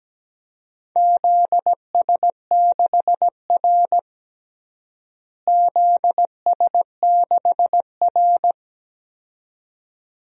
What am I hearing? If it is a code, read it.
Morse code, heard as a keyed tone, about 17 wpm, 700 Hz. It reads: ZS6R ZS6R
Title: DXCC-100 Call Signs - 4 Characters - Repeated 2x - 17wpm